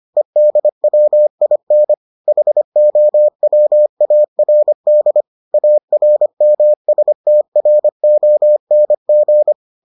Morse code, heard as a keyed tone, about 25 wpm, 600 Hz. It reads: EDWIN HOWARD ARMSTRONG